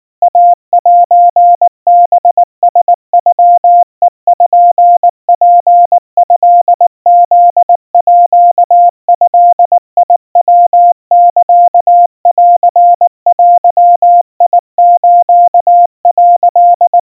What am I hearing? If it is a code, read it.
Morse code (Japanese Wabun code), heard as a keyed tone, about 19 wpm, 700 Hz. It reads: イセハラノペツトフードヤサンデス」
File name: kawa / Morse sample / kamyu 70 cpm 700 Hz Crispy